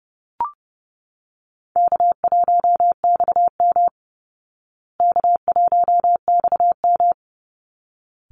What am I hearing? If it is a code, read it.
Morse code, heard as a keyed tone, about 30 wpm, 700 Hz. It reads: K1XM K1XM
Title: Easy Call Signs Repeated 2x - 30wpm